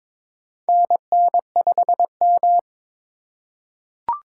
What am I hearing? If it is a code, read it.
Morse code, heard as a keyed tone, about 22 wpm, 700 Hz. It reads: NN5M
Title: USA Call Signs - 4 Characters - 2x Character Spacing - 22wpm